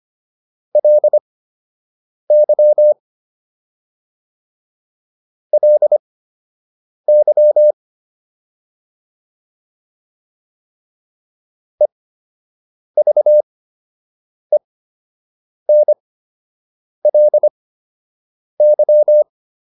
Morse code, heard as a keyed tone, about 25 wpm, spaced as for 7 wpm, 600 Hz. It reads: LY LY EVENLY